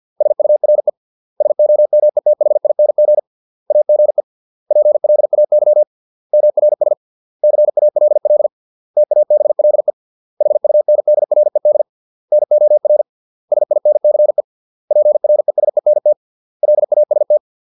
Morse code, heard as a keyed tone, about 50 wpm, 600 Hz. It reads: SURE SOMETHING AGE PLAY MRS CALL TABLE HUNDRED NOR SINCE PRESENT LAST